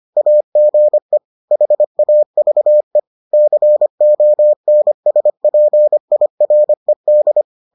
Morse code, heard as a keyed tone, about 25 wpm, 600 Hz. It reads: AGE HAVE CONSPIRED